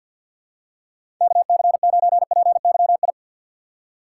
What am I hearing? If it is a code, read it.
Morse code, heard as a keyed tone, about 46 wpm, 700 Hz. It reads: KC9PYI
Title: USA Call Signs - 4x Character Spacing - 50wpm